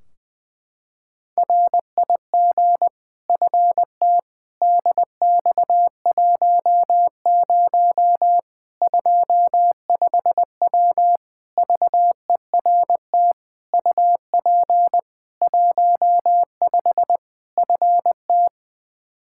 Morse code, heard as a keyed tone, about 20 wpm, 700 Hz. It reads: RIG FT DX10 25W VERT UP 15 FT
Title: K1USN SST QSO Practice 20wpm